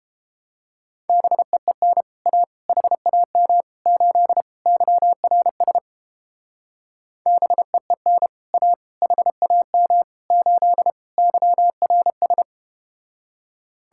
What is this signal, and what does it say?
Morse code, heard as a keyed tone, about 31 wpm, 700 Hz. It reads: BEEN A HAM 8 YRS BEEN A HAM 8 YRS